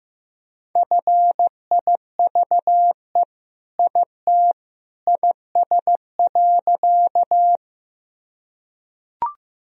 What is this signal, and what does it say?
Morse code, heard as a keyed tone, about 16 wpm, 700 Hz. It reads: FIVE IT IS.
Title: Sentences from Top 1000 Words - Review Part 2 - 15wpm